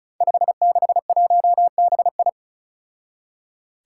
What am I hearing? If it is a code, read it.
Morse code, heard as a keyed tone, about 34 wpm, 700 Hz. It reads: 561BI
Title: Five Challenging Letter-Number Combinations 35wpm